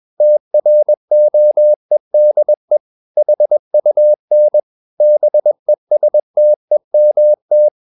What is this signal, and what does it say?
Morse code, heard as a keyed tone, about 21 wpm, 600 Hz. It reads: TROEDE HUN BESTEMT
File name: Ferie QRS 21wpm